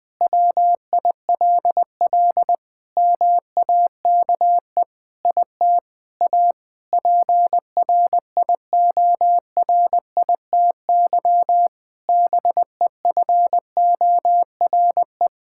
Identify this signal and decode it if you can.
Morse code, heard as a keyed tone, about 20 wpm, 700 Hz. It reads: WILL MAKE IT A PRIORITY BEFORE
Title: Sentences from Top 1600 Words 20wpm